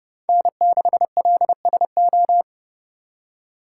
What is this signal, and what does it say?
Morse code, heard as a keyed tone, about 29 wpm, 700 Hz. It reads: N6LSO